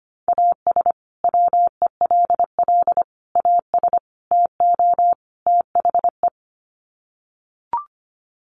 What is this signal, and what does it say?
Morse code, heard as a keyed tone, about 25 wpm, 700 Hz. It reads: AS WELL AS TO THE
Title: Sets of 5 Words Encore2 25wpm